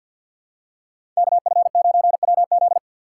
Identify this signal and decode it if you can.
Morse code, heard as a keyed tone, about 50 wpm, 700 Hz. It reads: KF9WZ
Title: USA Call Signs - 5 Characters - 4x Character Spacing - 50wpm